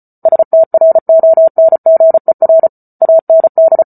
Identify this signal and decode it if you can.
Morse code, heard as a keyed tone, about 34 wpm, 650 Hz. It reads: STRONGER AND